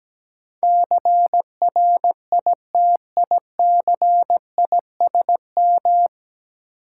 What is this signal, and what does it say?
Morse code, heard as a keyed tone, about 17 wpm, 700 Hz. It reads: CRITICISM
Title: Sentences from Top 1300 Words - New Word Drill - 17wpm